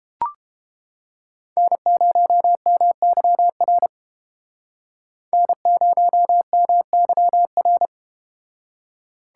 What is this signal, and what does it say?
Morse code, heard as a keyed tone, about 31 wpm, 700 Hz. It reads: N0MYR N0MYR